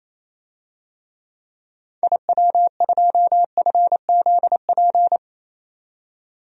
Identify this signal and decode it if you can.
Morse code, heard as a keyed tone, about 28 wpm, 700 Hz. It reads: IW2FZP
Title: DXCC-100 Call Signs - 6 Characters - 28wpm